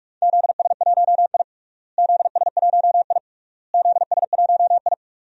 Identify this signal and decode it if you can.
Morse code, heard as a keyed tone, about 46 wpm, 700 Hz. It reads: ZS1I ZS1I ZS1I
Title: DXCC-100 Call Signs - 4 Characters - Repeated 3x - 45wpm